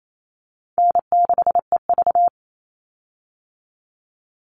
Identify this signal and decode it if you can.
Morse code, heard as a keyed tone, about 28 wpm, 700 Hz. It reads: N6EV
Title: Easy Call Signs Repeated 2x - 28wpm